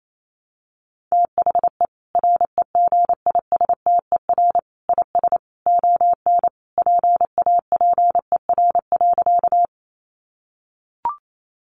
Morse code, heard as a keyed tone, about 28 wpm, 700 Hz. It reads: THE REGISTER IS ON PAPER.
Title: Sentences from Top 1200 Words 28wpm